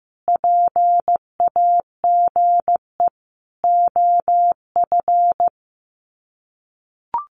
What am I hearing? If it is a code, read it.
Morse code, heard as a keyed tone, about 15 wpm, 700 Hz. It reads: PAGE OF
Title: Sets of 5 Words Encore 15wpm